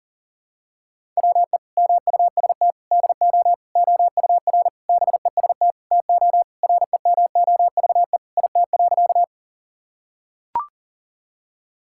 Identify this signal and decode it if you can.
Morse code, heard as a keyed tone, about 40 wpm, 700 Hz. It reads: WE MUST DO OUR BEST TO REMOVE IT.
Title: Sentences from Top 1000 Words - Review Part 5 - 40wpm